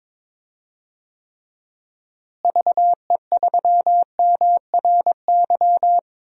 Morse code, heard as a keyed tone, about 22 wpm, 700 Hz. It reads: VE3MRY
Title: DXCC-100 Call Signs - 6 Characters - Repeated 3x - 22wpm